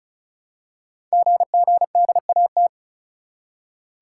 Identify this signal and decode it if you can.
Morse code, heard as a keyed tone, about 36 wpm, 700 Hz. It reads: GGDAT
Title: Code Groups - 5 Letters - 35wpm